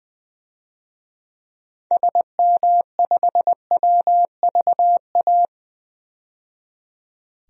Morse code, heard as a keyed tone, about 20 wpm, 700 Hz. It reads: SM5WVA
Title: DXCC-100 Call Signs - 6 Characters - 20wpm